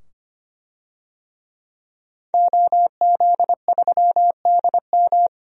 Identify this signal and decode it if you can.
Morse code, heard as a keyed tone, about 25 wpm, 700 Hz. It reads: OZ3DM
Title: DXCC-100 Call Signs - 5 Characters - 25wpm